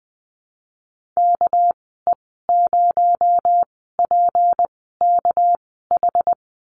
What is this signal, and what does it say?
Morse code, heard as a keyed tone, about 20 wpm, spaced as for 14 wpm, 700 Hz. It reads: KE0PKH
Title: 231 Lesson 37g - Letter x - Call Signs 20wpm